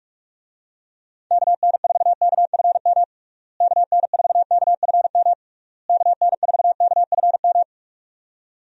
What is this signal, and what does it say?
Morse code, heard as a keyed tone, about 42 wpm, 700 Hz. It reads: KN4KFM KN4KFM KN4KFM
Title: USA Call Signs - 6 Characters - Repeated 3x - 45wpm